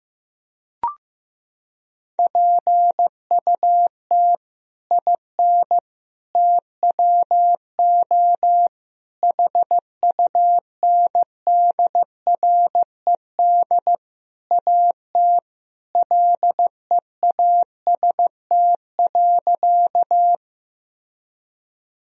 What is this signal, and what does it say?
Morse code, heard as a keyed tone, about 16 wpm, 700 Hz. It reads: PUT IN TWO HUNDRED AT LEAST.
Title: Sentences from Top 1000 Words - Review Part 2 - 15wpm